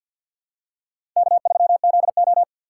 Morse code, heard as a keyed tone, about 53 wpm, 690 Hz. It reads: K3ZO
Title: USA Call Signs - 4 Characters - 6x Character Spacing - 50wpm